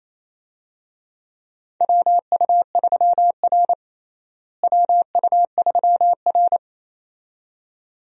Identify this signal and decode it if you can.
Morse code, heard as a keyed tone, about 28 wpm, 700 Hz. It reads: WU3R WU3R
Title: USA Call Signs - 4 Characters - Repeated 2x - 28wpm